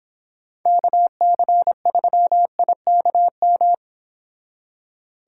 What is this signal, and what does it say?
Morse code, heard as a keyed tone, about 30 wpm, 700 Hz. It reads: KC3IKM